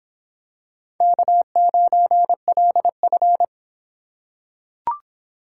Morse code, heard as a keyed tone, about 26 wpm, 700 Hz. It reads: K9LF